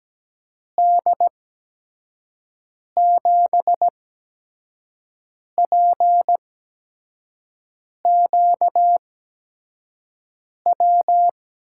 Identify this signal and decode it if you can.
Morse code, heard as a keyed tone, about 17 wpm, spaced as for 5 wpm, 700 Hz. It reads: D7PQW
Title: USA Call Signs - 6 Characters - 8x Character Spacing - 17wpm